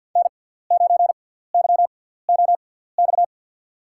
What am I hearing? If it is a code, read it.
Morse code, heard as a keyed tone, about 53 wpm, spaced as for 17 wpm, 700 Hz. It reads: N9YOX
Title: USA Call Signs - 5 Characters - 6x Character Spacing - 50wpm